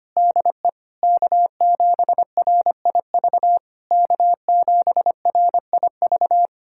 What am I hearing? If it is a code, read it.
Morse code, heard as a keyed tone, about 25 wpm, 700 Hz. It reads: DE K7RIV K7RIV
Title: SOTA QSO Practice 25wpm